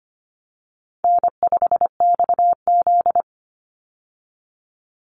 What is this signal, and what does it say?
Morse code, heard as a keyed tone, about 25 wpm, 700 Hz. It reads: N5XZ
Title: Easy Call Signs - 25wpm